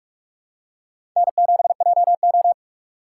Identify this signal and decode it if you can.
Morse code, heard as a keyed tone, about 46 wpm, 700 Hz. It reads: N7JO
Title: USA Call Signs - 4 Characters - 4x Character Spacing - 45wpm